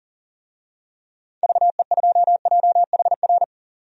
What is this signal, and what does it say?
Morse code, heard as a keyed tone, about 40 wpm, 700 Hz. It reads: VE2JHR
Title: DXCC-100 Call Signs - 6 Characters - 40wpm